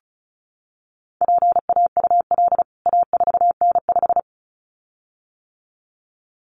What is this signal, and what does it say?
Morse code, heard as a keyed tone, about 34 wpm, 700 Hz. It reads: PAUL A4N5
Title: CWT Exchange 35wpm